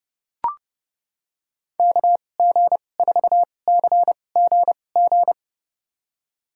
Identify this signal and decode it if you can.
Morse code, heard as a keyed tone, about 29 wpm, spaced as for 21 wpm, 700 Hz. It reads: KG4CGG
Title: USA Call Signs - 6 Characters - 2x Character Spacing - 30wpm